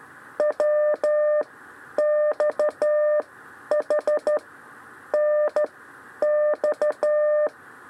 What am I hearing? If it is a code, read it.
Morse code, heard as a keyed tone, about 12 wpm, spaced as for 10 wpm, 600 Hz. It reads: W XH N X
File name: 6 lezione cw 2